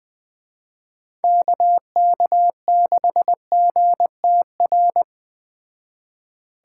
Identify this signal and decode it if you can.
Morse code, heard as a keyed tone, about 20 wpm, 700 Hz. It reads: KK6GTR